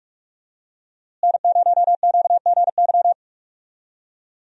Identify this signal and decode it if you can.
Morse code, heard as a keyed tone, about 46 wpm, 700 Hz. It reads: N0QGY
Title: USA Call Signs - 5 Characters - Repeated 2x - 45wpm